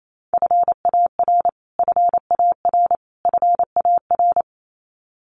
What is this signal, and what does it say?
Morse code, heard as a keyed tone, about 28 wpm, 700 Hz. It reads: FAR FAR FAR
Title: Top 500 Words Repeated 28wpm